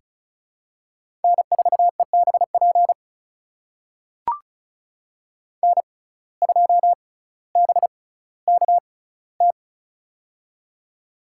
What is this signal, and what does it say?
Morse code, heard as a keyed tone, about 36 wpm, spaced as for 18 wpm, 700 Hz. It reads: N4EBP N2BKT
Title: USA Call Signs - 5 Characters - 6x Character Spacing - 35wpm